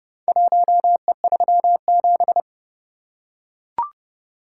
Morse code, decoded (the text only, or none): 1E37